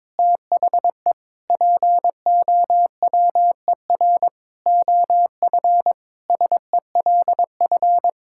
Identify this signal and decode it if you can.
Morse code, heard as a keyed tone, about 22 wpm, 700 Hz. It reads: THE POWER OF SELF
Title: Sentences from Top 1000 Words - Review Part 5 - 22wpm